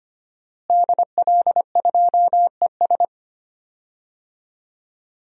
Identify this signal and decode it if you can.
Morse code, heard as a keyed tone, about 25 wpm, 700 Hz. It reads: DL2ES
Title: DXCC-100 Call Signs - 5 Characters - Repeated 2x - 25wpm